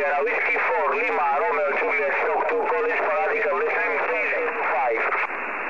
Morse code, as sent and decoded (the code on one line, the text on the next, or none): none